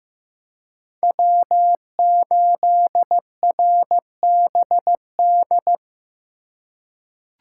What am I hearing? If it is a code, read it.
Morse code, heard as a keyed tone, about 15 wpm, 700 Hz. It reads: W8RBD
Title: USA Call Signs - 5 Characters - 6x Character Spacing - 15wpm